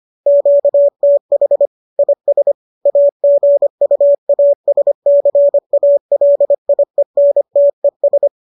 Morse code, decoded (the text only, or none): QTH IS AGUASCALIENTES